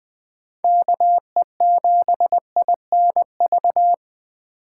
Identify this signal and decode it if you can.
Morse code, heard as a keyed tone, about 21 wpm, 700 Hz. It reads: KE7INV